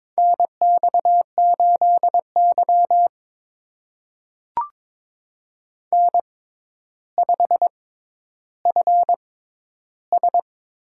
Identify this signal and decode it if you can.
Morse code, heard as a keyed tone, about 21 wpm, spaced as for 11 wpm, 700 Hz. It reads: NX8Y N5FS
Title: USA Call Signs - 4 Characters - 6x Character Spacing - 22wpm